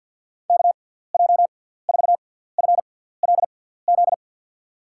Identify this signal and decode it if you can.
Morse code, heard as a keyed tone, about 48 wpm, spaced as for 17 wpm, 700 Hz. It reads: KJ4FLZ
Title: USA Call Signs - 6 Characters - 6x Character Spacing - 50wpm